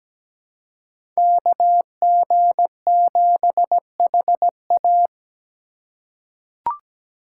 Morse code, heard as a keyed tone, about 17 wpm, 700 Hz. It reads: KG7HA
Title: USA Call Signs - 5 Characters - 3x Character Spacing - 17wpm